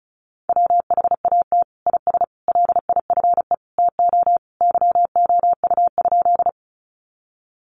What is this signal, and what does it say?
Morse code, heard as a keyed tone, about 34 wpm, 700 Hz. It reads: WHAT IS LIFE TO YOU?
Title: Sentences from Top 300 Words 35wpm